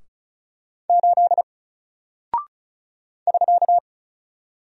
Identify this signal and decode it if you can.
Morse code, heard as a keyed tone, about 34 wpm, 700 Hz. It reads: 8 <SK>
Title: Single Character (Letter:Number:Punctuation:Prosign) 35wpm